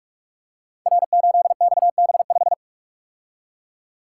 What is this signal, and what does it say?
Morse code, heard as a keyed tone, about 44 wpm, 700 Hz. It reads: R8XB5
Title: Code Groups - 5 Letter:Numbers - 45wpm